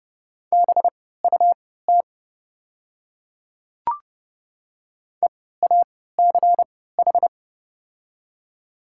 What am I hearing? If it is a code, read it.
Morse code, heard as a keyed tone, about 30 wpm, spaced as for 17 wpm, 700 Hz. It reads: BUT EACH
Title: 3x Character Spacing - Top 200 Words - 30wpm